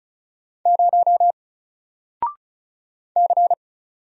Morse code, heard as a keyed tone, about 36 wpm, 700 Hz. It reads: 0 C
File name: Single-Letter-Number-35wpm